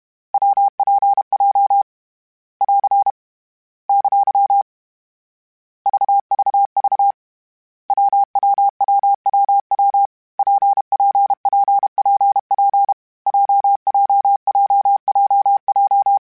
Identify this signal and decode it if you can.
Morse code, heard as a keyed tone, about 32 wpm, 800 Hz. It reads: WPJ <AR> ! VVV WWWWW PPPPP JJJJJ
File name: Variable Speed / Variable Speed-mp3s / WPJ WPJ1 sp @30